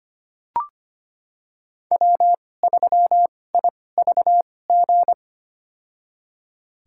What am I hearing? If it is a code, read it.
Morse code, heard as a keyed tone, about 25 wpm, spaced as for 18 wpm, 700 Hz. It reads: W3IVG
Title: USA Call Signs - 2x Character Spacing - 25wpm